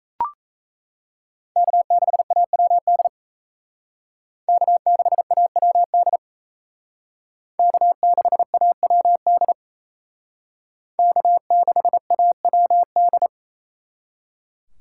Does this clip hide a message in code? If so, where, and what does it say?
Yes: Morse code from 1.6 to 13.3 s, heard as a keyed tone, about 35 wpm, 700 Hz. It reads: K6AWD K6AWD K6AWD K6AWD